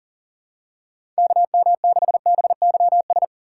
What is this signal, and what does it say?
Morse code, heard as a keyed tone, about 37 wpm, 700 Hz. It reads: KM6BYS
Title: USA Call Signs - 6 Characters - 4x Character Spacing - 40wpm